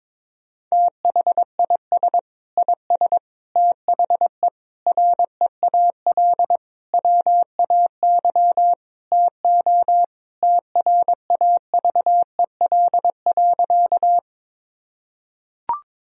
Morse code, heard as a keyed tone, about 22 wpm, 700 Hz. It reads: THIS IS THE REAL WAY TO TRAVEL.
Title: Sentences from Top 1000 Words - Review Part 2 - 22wpm